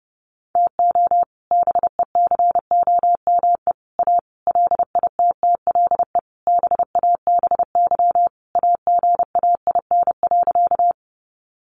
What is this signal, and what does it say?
Morse code, heard as a keyed tone, about 30 wpm, 700 Hz. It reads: TO BECOME A LITTLE BABY AGAIN.
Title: Sentences from Top 1300 Words 30wpm